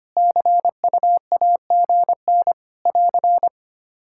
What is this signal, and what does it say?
Morse code, heard as a keyed tone, about 25 wpm, 700 Hz. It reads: CUAGN <AR>
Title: K1USN SST QSO Practice 25wpm